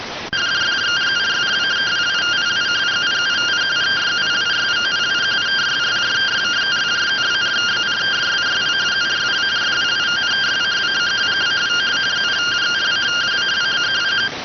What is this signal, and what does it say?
Two-tone teleprinter signal, about 170 Hz shift.